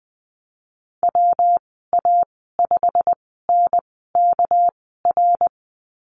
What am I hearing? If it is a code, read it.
Morse code, heard as a keyed tone, about 20 wpm, spaced as for 14 wpm, 700 Hz. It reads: WA5NKR
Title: 219 Lesson 35g - Number 0 - Call Signs 20wpm